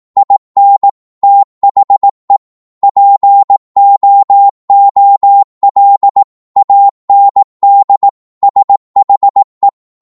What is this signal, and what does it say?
Morse code, heard as a keyed tone, about 18 wpm, 800 Hz. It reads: IN THE POOL AND SHE
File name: Morse / psg-18wpm